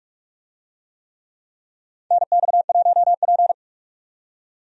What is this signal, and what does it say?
Morse code, heard as a keyed tone, about 46 wpm, 700 Hz. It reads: NX1P
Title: USA Call Signs - 4 Characters - 45wpm